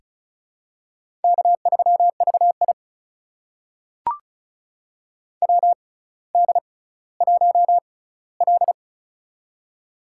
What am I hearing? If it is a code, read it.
Morse code, heard as a keyed tone, about 33 wpm, spaced as for 18 wpm, 700 Hz. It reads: K3VI WD1L
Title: USA Call Signs - 4 Characters - 6x Character Spacing - 35wpm